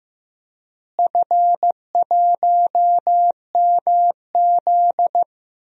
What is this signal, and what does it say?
Morse code, heard as a keyed tone, about 15 wpm, 700 Hz. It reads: F1MZ
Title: DXCC-100 Call Signs - 4 Characters - Repeated 3x - 15wpm